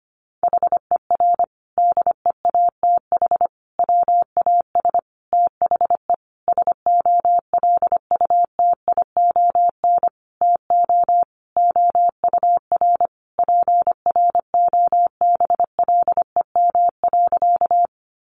Morse code, heard as a keyed tone, about 25 wpm, 700 Hz. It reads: HER DEATH WAS THE SOLUTION TO OUR PROBLEM.
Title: Sentences from Top 1300 Words 25wpm